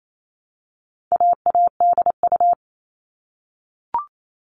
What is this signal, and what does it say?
Morse code, heard as a keyed tone, about 28 wpm, 700 Hz. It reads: AADU
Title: All Cut Numbers - 4-Digit 28wpm